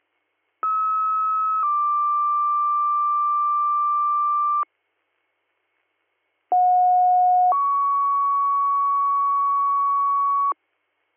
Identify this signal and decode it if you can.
Two-tone page, heard as a one-second tone, then a three-second tone, about 1290 then 1180 Hz, about 730 then 1120 Hz. It is heard from 0.6 to 10.5 s.